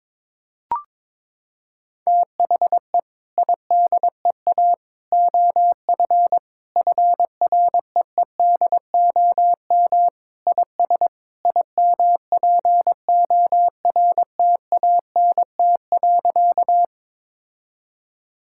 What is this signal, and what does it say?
Morse code, heard as a keyed tone, about 22 wpm, 700 Hz. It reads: THE IDEA OF FREEDOM IS IMPORTANT.
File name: practice / 22wpm / Sentences with 3-Word Phrases - 22wpm